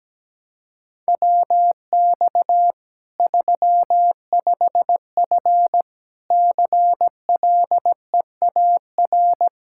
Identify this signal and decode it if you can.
Morse code, heard as a keyed tone, about 17 wpm, 700 Hz. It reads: WX 35F CLEAR
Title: K1USN SST QSO Practice 17wpm